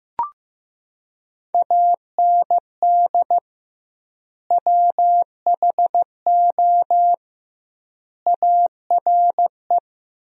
Morse code, heard as a keyed tone, about 15 wpm, 700 Hz. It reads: AND WHO ARE
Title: 2x Word-Spacing - Sentences from Top 100 Words - 15wpm